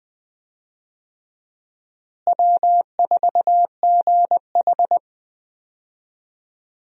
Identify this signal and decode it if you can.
Morse code, heard as a keyed tone, about 20 wpm, 700 Hz. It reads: W4GH
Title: USA Call Signs - 4 Characters - Repeated 3x - 20wpm